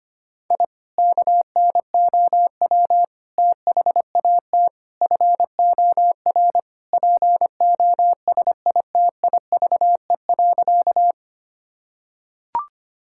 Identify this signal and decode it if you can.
Morse code, heard as a keyed tone, about 25 wpm, 700 Hz. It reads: I KNOW THAT FOR POSITIVE.
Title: Sentences from Top 1000 Words - Review Part 5 - 25wpm